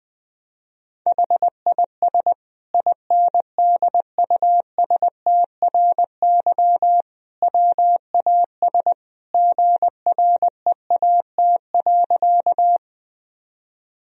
Morse code, heard as a keyed tone, about 20 wpm, 700 Hz. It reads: HIS INDUSTRY WAS GREAT.
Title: Sentences from Top 1000 Words - Review Part 4 - 20wpm